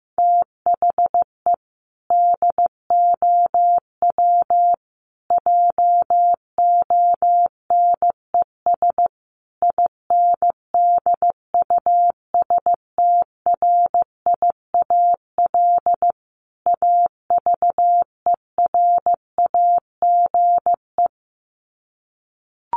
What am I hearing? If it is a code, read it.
Morse code, heard as a keyed tone, about 15 wpm, 700 Hz. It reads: THE DOW JONES INDUSTRIAL AVERAGE
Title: Sets of 5 Words 15wpm